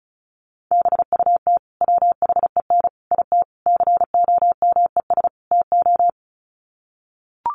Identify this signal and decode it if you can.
Morse code, heard as a keyed tone, about 34 wpm, 700 Hz. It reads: BUT WHEN IT COMES TO
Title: Sets of 5 Words 35wpm